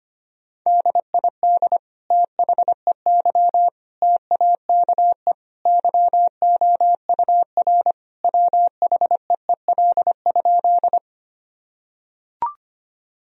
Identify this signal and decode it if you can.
Morse code, heard as a keyed tone, about 25 wpm, 700 Hz. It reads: DID THEY TAKE YOUR WHEEL?
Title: Sentences from Top 1000 Words - Review Part 3 - 25wpm